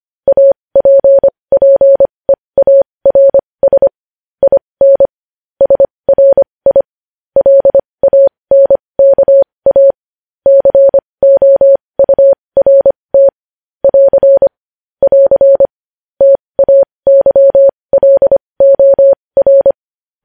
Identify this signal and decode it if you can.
Morse code, heard as a keyed tone, about 25 wpm, spaced as for 20 wpm, 550 Hz. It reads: APPEARS IN SRI LANKA COURT <AR> <AR> TAYLOR
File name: podcast / News headlines-1x-25x20